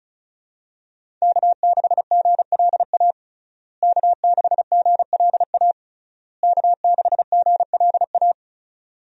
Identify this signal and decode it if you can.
Morse code, heard as a keyed tone, about 36 wpm, 700 Hz. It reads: K6GLA K6GLA K6GLA
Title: USA Call Signs - 5 Characters - Repeated 3x - 35wpm